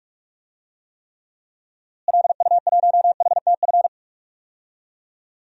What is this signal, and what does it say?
Morse code, heard as a keyed tone, about 42 wpm, 700 Hz. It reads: LU1HTF
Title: DXCC-100 Call Signs - 6 Characters - Repeated 2x - 45wpm